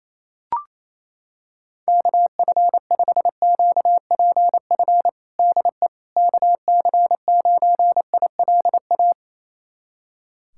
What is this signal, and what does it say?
Morse code, heard as a keyed tone, about 28 wpm, 700 Hz. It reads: KF5QPF DE KC9ILA
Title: USA Callsign DE Callsign 28wpm